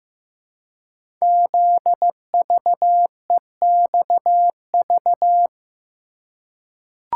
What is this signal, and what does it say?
Morse code, heard as a keyed tone, about 15 wpm, 700 Hz. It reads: ZVEXV
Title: Code Groups - 5 Letters - 15wpm